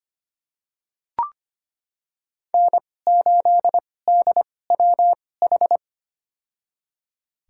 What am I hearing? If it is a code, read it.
Morse code, heard as a keyed tone, about 25 wpm, spaced as for 18 wpm, 700 Hz. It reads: N8DWH
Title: USA Call Signs - 5 Characters - 2x Character Spacing - 25wpm